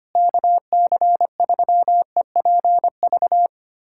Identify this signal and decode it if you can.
Morse code, heard as a keyed tone, about 25 wpm, 700 Hz. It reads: KC3EPV
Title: USA Callsign DE Callsign - 6 Characters - 25wpm